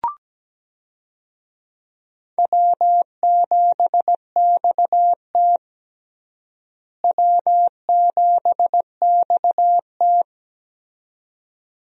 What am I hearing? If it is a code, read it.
Morse code, heard as a keyed tone, about 17 wpm, 700 Hz. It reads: W7XT W7XT
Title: USA Call Signs - 4 Characters - Repeated 2x - 17wpm